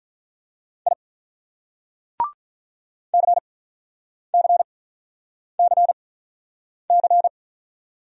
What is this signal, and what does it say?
Morse code, heard as a keyed tone, about 43 wpm, spaced as for 9 wpm, 700 Hz. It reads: I CCCC
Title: Speed-Racing Single Letters 35wpm